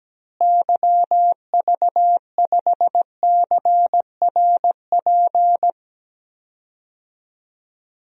Morse code, heard as a keyed tone, about 17 wpm, 700 Hz. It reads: YV5CRP
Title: DXCC-100 Call Signs - 6 Characters - Repeated 3x - 17wpm